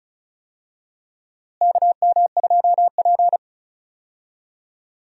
Morse code, heard as a keyed tone, about 36 wpm, 700 Hz. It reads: KM2P
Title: USA Call Signs - 4 Characters - Repeated 3x - 35wpm